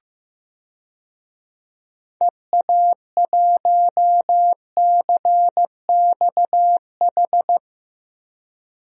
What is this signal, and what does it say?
Morse code, heard as a keyed tone, about 15 wpm, 700 Hz. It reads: EA1CXH
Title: DXCC-100 Call Signs - 6 Characters - Repeated 3x - 15wpm